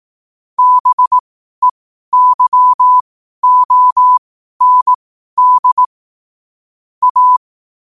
Morse code, heard as a keyed tone, about 18 wpm, spaced as for 12 wpm, 1000 Hz. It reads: BEYOND A
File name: qotdfiles / 12wpm